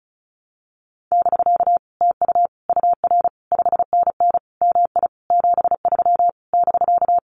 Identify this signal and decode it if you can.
Morse code, heard as a keyed tone, about 34 wpm, 700 Hz. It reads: <BK> TU UR 5NN MI 73 <BK>
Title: Common QSO Phrases Encore 35wpm